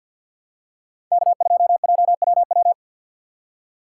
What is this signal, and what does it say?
Morse code, heard as a keyed tone, about 52 wpm, 700 Hz. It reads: K2JWW
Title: USA Call Signs - 2x Character Spacing - 50wpm